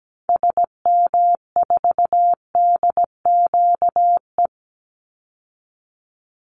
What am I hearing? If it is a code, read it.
Morse code, heard as a keyed tone, about 17 wpm, 700 Hz. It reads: SM4DQE
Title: Call Signs Repeated 3x - 17wpm